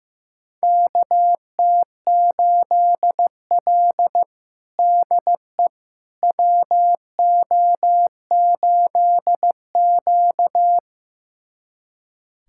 Morse code, heard as a keyed tone, about 15 wpm, 700 Hz. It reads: KT8L DE WO8Q
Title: USA Callsign DE Callsign - 4 Characters - 15wpm